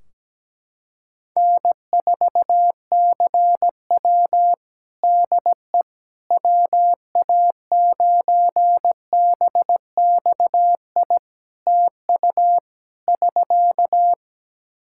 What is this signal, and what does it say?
Morse code, heard as a keyed tone, about 17 wpm, 700 Hz. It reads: N4CW DE WA9BXI TU <SK>
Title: Practice QSO - Exchange 4 - Ending - 17wpm